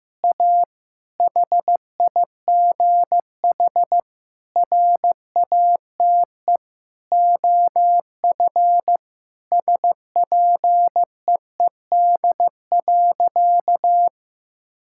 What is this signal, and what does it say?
Morse code, heard as a keyed tone, about 15 wpm, 700 Hz. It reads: A HIGH RATE OF SPEED.
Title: Sentences from Top 1000 Words - Review Part 5 - 15wpm